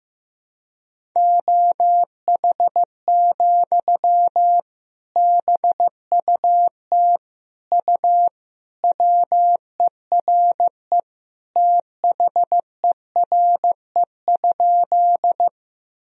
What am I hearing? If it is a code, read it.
Morse code, heard as a keyed tone, about 15 wpm, 700 Hz. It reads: OH, BUT U WERE THERE?
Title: Sentences from Top 500 Words and Abbreviations 15wpm